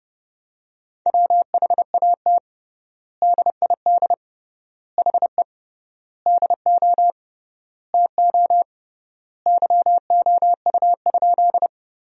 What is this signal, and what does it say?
Morse code, heard as a keyed tone, about 30 wpm, 700 Hz. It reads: WHAT DID HE DO TO YOU?
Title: 3x Word-Spacing - Sentences from Top 100 Words - 30wpm